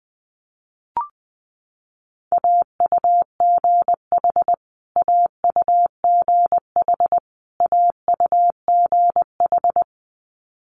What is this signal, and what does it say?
Morse code, heard as a keyed tone, about 19 wpm, 700 Hz. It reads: AUGH AUGH AUGH
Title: Multi-letter Phonemes Repeated 3x 20wpm